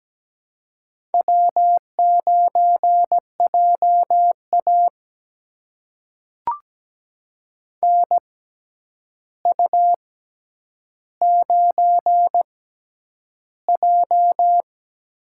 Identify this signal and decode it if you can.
Morse code, heard as a keyed tone, about 17 wpm, spaced as for 9 wpm, 700 Hz. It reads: W9JA NU9J
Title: USA Call Signs - 4 Characters - 6x Character Spacing - 17wpm